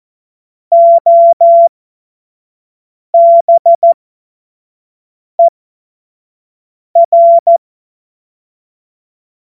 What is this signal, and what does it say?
Morse code, heard as a keyed tone, about 14 wpm, spaced as for 5 wpm, 680 Hz. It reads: OBER
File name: Morsekurs / Witze / Witz 29. Juni 2021, 25 F70